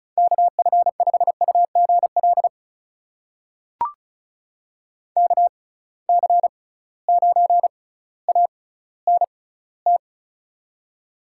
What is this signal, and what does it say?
Morse code, heard as a keyed tone, about 36 wpm, spaced as for 18 wpm, 700 Hz. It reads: KF5UGL KC9ANT
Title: USA Call Signs - 6 Characters - 6x Character Spacing - 35wpm